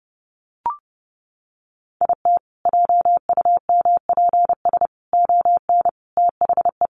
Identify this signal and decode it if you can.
Morse code, heard as a keyed tone, about 30 wpm, 700 Hz. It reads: IT JUMPS ON THE
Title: Additional Easy Phrases 30wpm